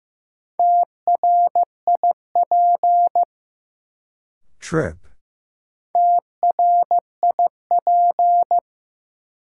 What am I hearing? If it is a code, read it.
Morse code, heard as a keyed tone, about 15 wpm, 700 Hz. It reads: TRIP TRIP
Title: Sentences from Top 1000 Words - Review Part 4 - New Word Drill - 15wpm